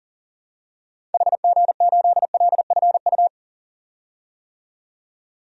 Morse code, heard as a keyed tone, about 40 wpm, 700 Hz. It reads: HG8LFU
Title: DXCC-100 Call Signs - 6 Characters - 40wpm